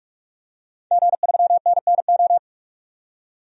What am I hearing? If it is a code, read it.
Morse code, heard as a keyed tone, about 41 wpm, 700 Hz. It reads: G3NNO
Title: DXCC-100 Call Signs - 5 Characters - 45wpm